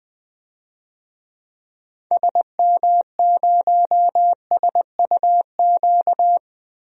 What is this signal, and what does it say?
Morse code, heard as a keyed tone, about 20 wpm, 700 Hz. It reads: SM0SUQ
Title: DXCC-100 Call Signs - 6 Characters - Repeated 3x - 20wpm